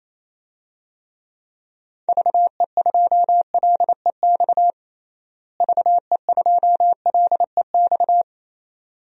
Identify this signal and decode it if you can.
Morse code, heard as a keyed tone, about 28 wpm, 700 Hz. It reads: VE2LEX VE2LEX
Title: DXCC-100 Call Signs - 6 Characters - Repeated 2x - 28wpm